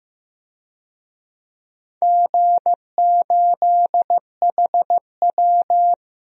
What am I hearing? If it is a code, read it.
Morse code, heard as a keyed tone, about 15 wpm, 700 Hz. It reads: G8HW
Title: DXCC-100 Call Signs - 4 Characters - 15wpm